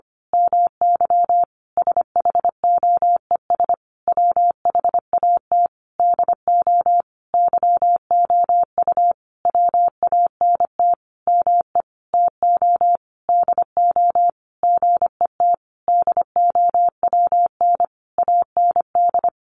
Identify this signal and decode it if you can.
Morse code, heard as a keyed tone, about 25 wpm, 700 Hz. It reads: MY SHOES WHAT DO YOU WANT ME TO DO GET DOWN AND
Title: A Sound Of Thunder 25wpm